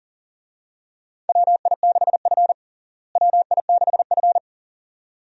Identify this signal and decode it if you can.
Morse code, heard as a keyed tone, about 40 wpm, 700 Hz. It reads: WI6F WI6F
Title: USA Call Signs - 4 Characters - Repeated 2x - 40wpm